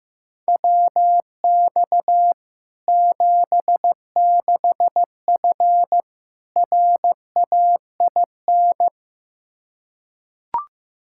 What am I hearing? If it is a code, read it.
Morse code, heard as a keyed tone, about 15 wpm, 700 Hz. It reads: WX 76F RAIN
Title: Practice QSO - Exchange 2 - Info - 15wpm